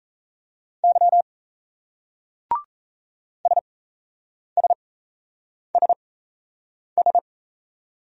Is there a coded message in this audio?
Yes: Morse code from 0.8 to 7.2 s, heard as a keyed tone, about 38 wpm, spaced as for 8 wpm, 700 Hz. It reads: Y SSSS